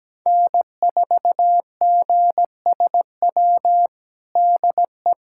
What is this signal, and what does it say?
Morse code, heard as a keyed tone, about 17 wpm, 700 Hz. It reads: N4GSW DE
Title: USA Callsign DE Callsign - 5 Characters - 17wpm